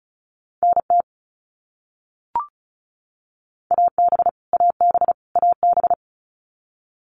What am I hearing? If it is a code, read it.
Morse code, heard as a keyed tone, about 34 wpm, 700 Hz. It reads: NT AB AB AB
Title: Canadian Provinces and Territories - Repeated 3x - 35wpm